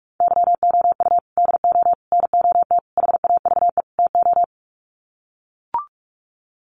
Morse code, heard as a keyed tone, about 44 wpm, 700 Hz. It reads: YOU DO NOT HAVE TO
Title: Sets of 5 Words 45wpm